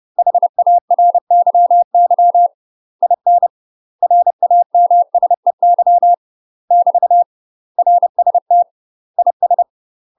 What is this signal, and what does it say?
Morse code, heard as a keyed tone, about 30 wpm, 700 Hz. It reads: HARYY IN RAMSEY <BT> RST IS
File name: UK MASTER QSOs TRAINING / 30WPM / UK MASTER QSOs 0017